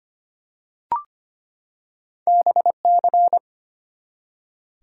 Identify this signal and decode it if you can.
Morse code, heard as a keyed tone, about 25 wpm, 700 Hz. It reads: BC
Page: morse-code-ninja-practice-25wpm